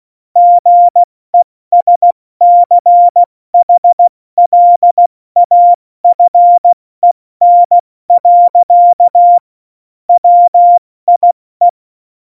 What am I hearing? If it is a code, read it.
Morse code, heard as a keyed tone, about 16 wpm, 700 Hz. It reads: GESCHLAFEN. WIE
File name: Morsekurs / Witze / Witz 01. September 2020, 70 F80